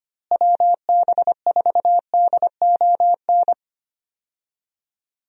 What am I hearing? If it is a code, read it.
Morse code, heard as a keyed tone, about 25 wpm, 700 Hz. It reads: WB4DON
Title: USA Call Signs - 6 Characters - Repeated 2x - 25wpm